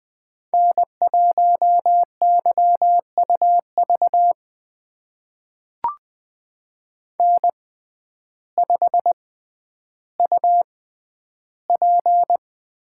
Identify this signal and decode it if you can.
Morse code, heard as a keyed tone, about 20 wpm, spaced as for 11 wpm, 700 Hz. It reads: N1YUV N5UP
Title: USA Call Signs - 6x Character Spacing - 20wpm